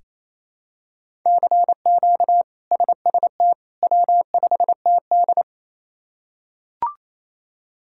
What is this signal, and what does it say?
Morse code, heard as a keyed tone, about 28 wpm, 700 Hz. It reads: CQ SST W5TD
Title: K1USN SST QSO Practice - 28wpm